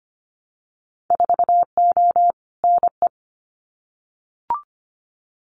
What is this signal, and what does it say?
Morse code, heard as a keyed tone, about 25 wpm, 700 Hz. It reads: 4O NE
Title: Winter Field Day Exchanges Encore 25wpm